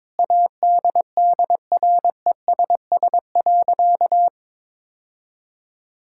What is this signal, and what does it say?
Morse code, heard as a keyed tone, about 22 wpm, 700 Hz. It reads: ADDRESS.
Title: Sentences from Top 1000 Words - Review Part 6 - 22wpm